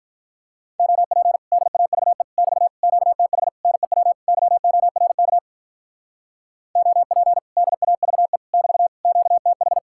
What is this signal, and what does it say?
Morse code, heard as a keyed tone, about 43 wpm, 700 Hz. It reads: OP DAVE <BT> QTH NEW YORK OP DAVE <BT> QTH
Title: Speed-Racing Common QSO Phrases Encore - 35wpm